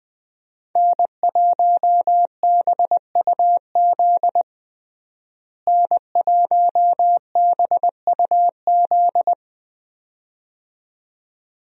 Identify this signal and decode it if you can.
Morse code, heard as a keyed tone, about 20 wpm, 700 Hz. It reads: N1BUZ N1BUZ
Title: USA Call Signs - 5 Characters - Repeated 2x - 20wpm